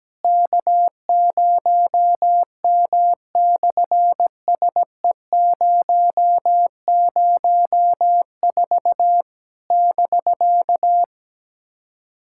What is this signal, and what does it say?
Morse code, heard as a keyed tone, about 17 wpm, 700 Hz. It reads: K0M/SE004 <BK>
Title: SOTA QSO Practice 17wpm